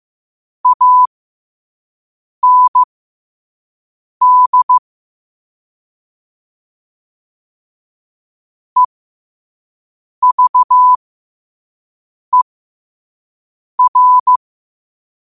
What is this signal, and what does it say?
Morse code, heard as a keyed tone, about 15 wpm, spaced as for 5 wpm, 1000 Hz. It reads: AND EVER
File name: qotdfiles / 5wpm